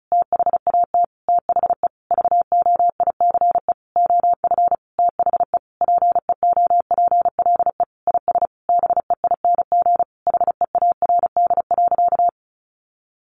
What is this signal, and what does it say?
Morse code, heard as a keyed tone, about 34 wpm, 700 Hz. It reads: THAT THE VOICE OF THE PEOPLE IS BEING HEARD.
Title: Sentences from Top 1500 Words 35wpm